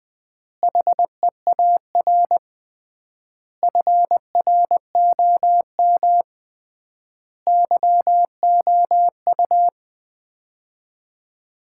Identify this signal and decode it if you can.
Morse code, heard as a keyed tone, about 20 wpm, 700 Hz. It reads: HEAR FROM YOU
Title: Common QSO Phrases - 3x Word-Spacing - 20wpm